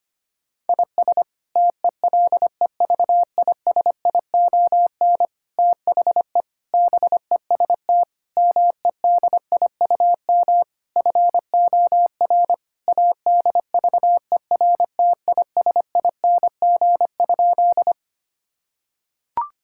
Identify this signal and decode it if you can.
Morse code, heard as a keyed tone, about 25 wpm, 700 Hz. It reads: IS TELEVISION THE BEST MEDIUM FOR ADVERTISING?
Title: Sentences from Top 1000 Words - Review Part 6 - 25wpm